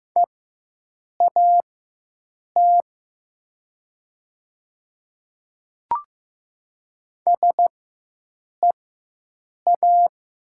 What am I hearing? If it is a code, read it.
Morse code, heard as a keyed tone, about 15 wpm, spaced as for 7 wpm, 700 Hz. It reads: EAT SEA
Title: 4x Character Spacing - Top 300 Words - 15wpm